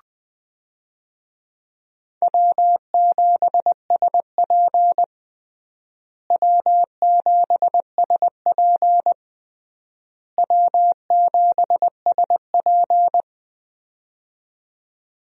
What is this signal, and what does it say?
Morse code, heard as a keyed tone, about 21 wpm, 700 Hz. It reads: W7SP W7SP W7SP
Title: USA Call Signs - 4 Characters - Repeated 3x - 20wpm